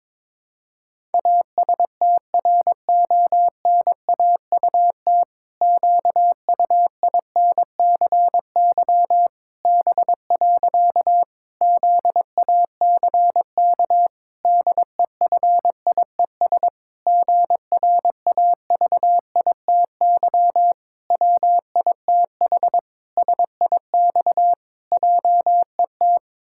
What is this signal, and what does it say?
Morse code, heard as a keyed tone, about 22 wpm, 700 Hz. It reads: ASTRONAUT QUINCY B. ZACK DEFIES GRAVITY WITH SIX JET
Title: Pangram Challenge v2 - 22wpm